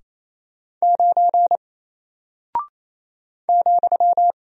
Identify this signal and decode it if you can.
Morse code, heard as a keyed tone, about 28 wpm, 700 Hz. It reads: 9 ,
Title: Single Character (Letter:Number:Punctuation:Prosign) Alt 28wpm